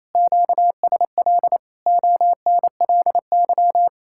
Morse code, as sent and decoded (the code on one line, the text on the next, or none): --.- ... .-.. / --- -. .-.. -.--
QSL ONLY